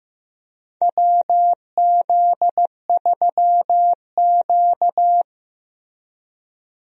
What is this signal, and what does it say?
Morse code, heard as a keyed tone, about 15 wpm, 700 Hz. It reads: WZ3Q
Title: USA Call Signs - 4 Characters - Repeated 2x - 15wpm